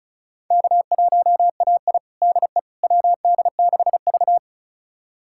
Morse code, heard as a keyed tone, about 34 wpm, 700 Hz. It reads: K1AI DE WD6V
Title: USA Callsign DE Callsign - 4 Characters - 35wpm